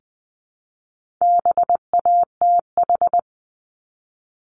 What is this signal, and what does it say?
Morse code, heard as a keyed tone, about 20 wpm, 700 Hz. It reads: BATH
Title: Multi-letter Phonemes Repeated 3x 20wpm